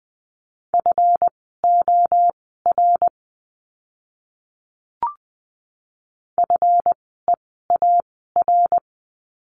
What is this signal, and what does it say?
Morse code, heard as a keyed tone, about 20 wpm, spaced as for 14 wpm, 700 Hz. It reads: FOR FEAR
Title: 139 Lesson 23f - Letter f - Words 20wpm